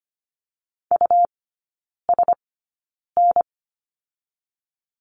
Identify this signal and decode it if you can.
Morse code, heard as a keyed tone, about 25 wpm, spaced as for 9 wpm, 700 Hz. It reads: USN
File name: Lesson 17e - Letter u - Three Characters 25wpm